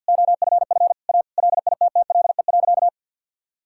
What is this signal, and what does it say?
Morse code, heard as a keyed tone, about 50 wpm, 700 Hz. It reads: OFF A LITTLE.
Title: Sentences from Top 1000 Words - Review Part 6 - 50wpm